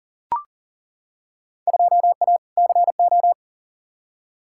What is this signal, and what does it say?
Morse code, heard as a keyed tone, about 40 wpm, 700 Hz. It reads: 2A CO
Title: Field Day Exchanges - 40wpm